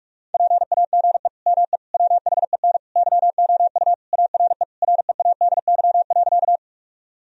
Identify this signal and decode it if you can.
Morse code, heard as a keyed tone, about 44 wpm, 700 Hz. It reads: PAGE ME WHEN YOU ARE READY.
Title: Sentences from Top 1000 Words - Review Part 1 - 45wpm